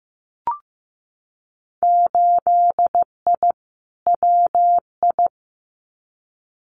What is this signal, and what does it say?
Morse code, heard as a keyed tone, about 15 wpm, 700 Hz. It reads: 8I WI
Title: Winter Field Day Exchanges 15wpm